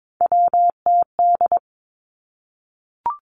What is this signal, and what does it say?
Morse code, heard as a keyed tone, about 22 wpm, 700 Hz. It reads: WTD
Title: Three Challenging Letter-Number Combinations 22wpm